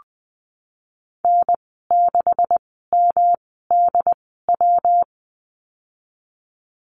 Morse code, heard as a keyed tone, about 20 wpm, spaced as for 15 wpm, 700 Hz. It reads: N6MDW
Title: 167 Lesson 27g - Number 7 - Call Signs 20wpm